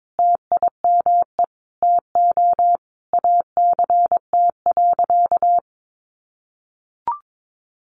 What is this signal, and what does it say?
Morse code, heard as a keyed tone, about 22 wpm, 700 Hz. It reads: TIME TO ACT.
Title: Sentences from Top 200 Words 22wpm